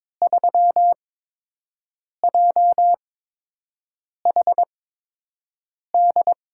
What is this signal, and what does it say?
Morse code, heard as a keyed tone, about 22 wpm, spaced as for 6 wpm, 700 Hz. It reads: 3JHD
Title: USA Call Signs - 6 Characters - 8x Character Spacing - 22wpm